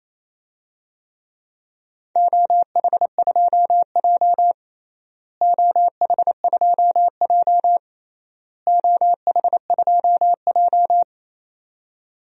Morse code, heard as a keyed tone, about 28 wpm, 700 Hz. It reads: OH2J OH2J OH2J
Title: DXCC-100 Call Signs - 4 Characters - Repeated 3x - 28wpm